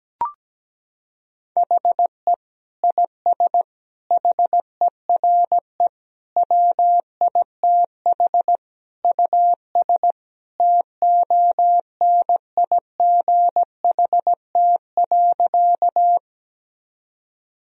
Morse code, heard as a keyed tone, about 17 wpm, 700 Hz. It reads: HE IS HERE WITH US TONIGHT.
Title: Sentences from Top 1000 Words - Review Part 3 - 17wpm